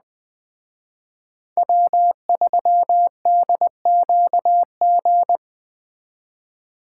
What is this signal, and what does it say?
Morse code, heard as a keyed tone, about 20 wpm, 700 Hz. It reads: W3DQG
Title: Speed Racing - USA 1x3 Call Signs 17wpm